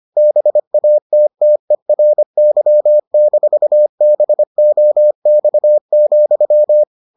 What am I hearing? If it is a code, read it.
Morse code, heard as a keyed tone, about 25 wpm, 600 Hz. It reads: BATTERY-BOX,